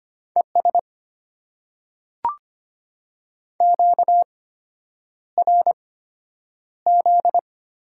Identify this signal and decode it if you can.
Morse code, heard as a keyed tone, about 25 wpm, spaced as for 9 wpm, 690 Hz. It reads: ES QRZ
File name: Top 90 QSO Elements - 8x Character Spacing - 25wpm